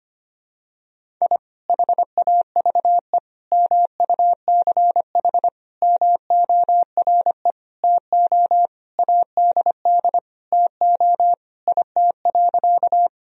Morse code, heard as a keyed tone, about 25 wpm, 700 Hz. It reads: I HAVE MUCH MORE TO ADD TO IT.
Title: Sentences from Top 1000 Words - Review Part 1 - 25wpm